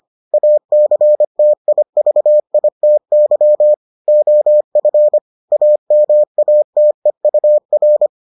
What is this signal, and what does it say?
Morse code, heard as a keyed tone, about 25 wpm, 600 Hz. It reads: ACTIVITY OF AMATEUR